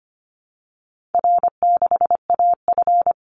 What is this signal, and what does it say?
Morse code, heard as a keyed tone, about 25 wpm, 700 Hz. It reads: R6AF
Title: Easy Call Signs Repeated 2x - 25wpm